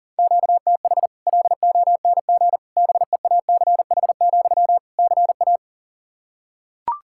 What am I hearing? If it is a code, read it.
Morse code, heard as a keyed tone, about 40 wpm, 700 Hz. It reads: QTH LONG BEACH, CA
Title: Practice QSO - Exchange 1 - Beginning - 40wpm